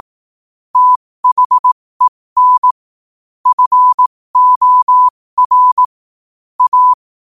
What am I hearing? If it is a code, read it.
Morse code, heard as a keyed tone, about 18 wpm, spaced as for 15 wpm, 1000 Hz. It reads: THEN FOR A
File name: qotdfiles / 15wpm